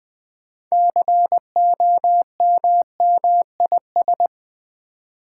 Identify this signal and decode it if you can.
Morse code, heard as a keyed tone, about 20 wpm, 700 Hz. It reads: COMMIS
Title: Build A Word - 6 Letters At A Time - 12-Letter Words - 20wpm